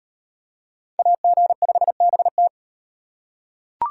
Morse code, heard as a keyed tone, about 33 wpm, 700 Hz. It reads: AG5BT